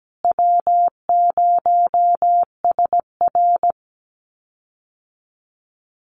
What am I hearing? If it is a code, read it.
Morse code, heard as a keyed tone, about 17 wpm, 700 Hz. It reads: W0SR
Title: USA Call Signs Repeated 3x 17wpm